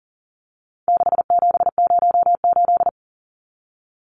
Morse code, heard as a keyed tone, about 40 wpm, 700 Hz. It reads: 6708
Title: Numbers - 4-Digit 40wpm